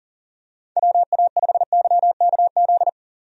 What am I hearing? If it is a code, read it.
Morse code, heard as a keyed tone, about 40 wpm, 690 Hz. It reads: WA5YKZ
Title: USA Call Signs - 6 Characters - 40wpm